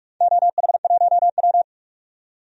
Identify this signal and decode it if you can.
Morse code, heard as a keyed tone, about 46 wpm, 700 Hz. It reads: OH1W
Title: DXCC-100 Call Signs - 4 Characters - 45wpm